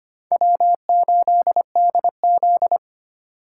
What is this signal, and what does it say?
Morse code, heard as a keyed tone, about 32 wpm, 700 Hz. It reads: W8DZ